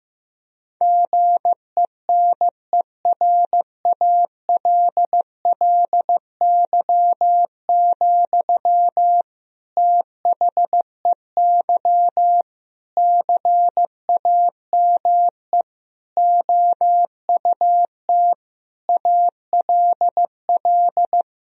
Morse code, heard as a keyed tone, about 15 wpm, 700 Hz. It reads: GENERALLY, THEY CAME OUT ALL
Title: Sentences from Top 1000 Words - Review Part 4 - 15wpm